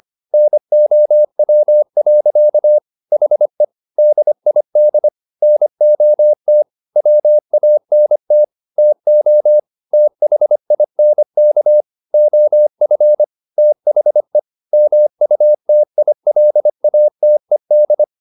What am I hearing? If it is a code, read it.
Morse code, heard as a keyed tone, about 25 wpm, 600 Hz. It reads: NOW. HE DID NOT WANT TO THINK OF THE MUTILATED